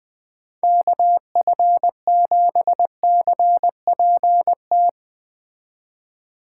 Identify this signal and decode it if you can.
Morse code, heard as a keyed tone, about 20 wpm, 700 Hz. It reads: KF7CPT